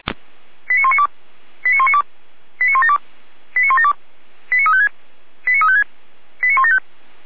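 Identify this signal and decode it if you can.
Five-tone selective call (ZVEI), heard as quick runs of five tones, tones of about 70 ms. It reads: 89192 89192 89172 89172 89356 89356 89175
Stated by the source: ZVEI five-tone selective call 89338, received by SDR.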